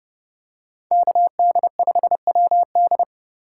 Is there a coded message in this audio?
Yes: Morse code from 0.9 to 3.0 s, heard as a keyed tone, about 29 wpm, 700 Hz. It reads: KD5WD